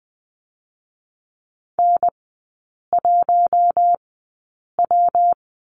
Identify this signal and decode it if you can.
Morse code, heard as a keyed tone, about 20 wpm, 700 Hz. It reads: N 1 W
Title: 127 Lesson 21e - Number 6 - Three Characters 20wpm